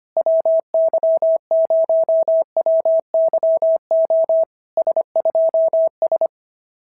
Morse code, heard as a keyed tone, about 24 wpm, 650 Hz. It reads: WY0WYO S2S
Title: SOTA QSO Practice 25wpm